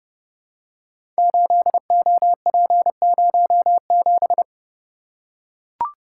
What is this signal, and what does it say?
Morse code, heard as a keyed tone, about 30 wpm, 700 Hz. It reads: 8OP07
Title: Code Groups - 5 Letter:Numbers - 30wpm